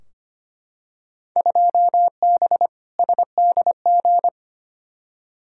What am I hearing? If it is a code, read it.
Morse code, heard as a keyed tone, about 25 wpm, 700 Hz. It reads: 2B SDG
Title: Field Day Exchanges Encore - 25wpm